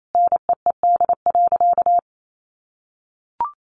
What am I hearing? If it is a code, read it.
Morse code, heard as a keyed tone, about 28 wpm, 700 Hz. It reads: NEED.
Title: Sentences from Top 1200 Words 28wpm